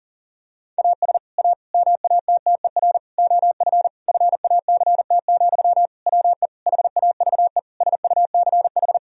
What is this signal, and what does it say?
Morse code, heard as a keyed tone, about 40 wpm, 700 Hz. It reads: AS A MATTER OF FACT, WE HAVE SUCH
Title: Sentences from Top 1000 Words - Review Part 3 - 40wpm